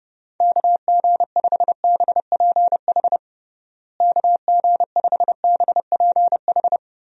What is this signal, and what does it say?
Morse code, heard as a keyed tone, about 32 wpm, 700 Hz. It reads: KG5BPH KG5BPH
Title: USA Call Signs - 6 Characters - Repeated 3x - 30wpm